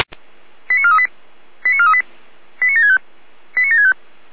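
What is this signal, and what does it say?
Five-tone selective call (ZVEI), heard as quick runs of five tones, tones of about 80 ms. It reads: 89438 89438 89765 89765
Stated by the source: ZVEI five-tone selective call received on an SDR.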